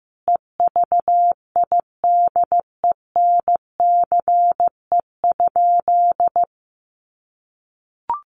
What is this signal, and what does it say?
Morse code, heard as a keyed tone, about 15 wpm, 700 Hz. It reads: EVIDENCE?
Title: Sentences from Top 1300 Words 15wpm